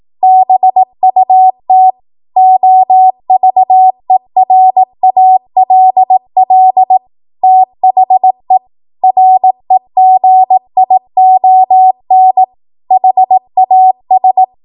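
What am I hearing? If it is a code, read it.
Morse code, heard as a keyed tone, about 18 wpm, 750 Hz. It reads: BUT OVERALL THE REGION HAS
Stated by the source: Code practice at 18 WPM, a transition run bridging 15 to 20 WPM.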